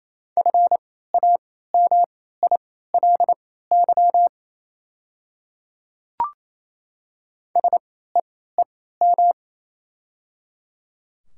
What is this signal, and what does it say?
Morse code, heard as a keyed tone, about 28 wpm, spaced as for 16 wpm, 700 Hz. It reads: FAMILY SEEM
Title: 3x Character Spacing - Top 500 Words - 28wpm